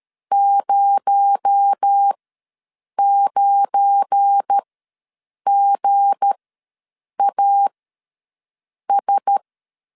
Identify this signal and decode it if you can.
Morse code, heard as a keyed tone, about 13 wpm, spaced as for 7 wpm, 800 Hz. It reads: 09GAS
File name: LEZIONI / 7 GF9Z / 7 GF9 ws 12wpm 3344